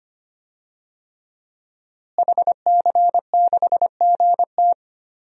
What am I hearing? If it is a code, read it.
Morse code, heard as a keyed tone, about 25 wpm, 700 Hz. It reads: HC6GT
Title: DXCC-100 Call Signs - 5 Characters - Repeated 3x - 25wpm